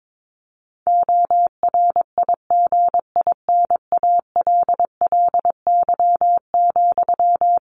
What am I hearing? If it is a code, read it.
Morse code, heard as a keyed tone, about 22 wpm, 700 Hz. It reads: ORIGINALLY,
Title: Sentences from Top 1500 Words 22wpm